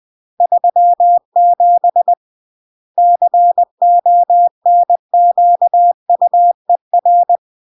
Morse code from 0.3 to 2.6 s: 37